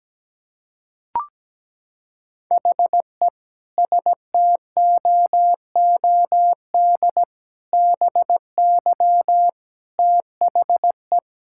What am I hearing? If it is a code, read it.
Morse code, heard as a keyed tone, about 17 wpm, 700 Hz. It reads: HE STOOD BY THE
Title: Sentences from Top 1000 Words - Review Part 6 - 17wpm